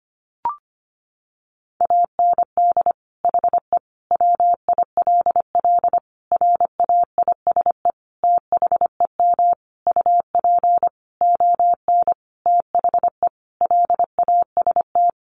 Morse code, heard as a keyed tone, about 25 wpm, 700 Hz. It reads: AND HE WILL RAISE THEM UP ON THE LAST
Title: Sentences from Top 1400 Words 25wpm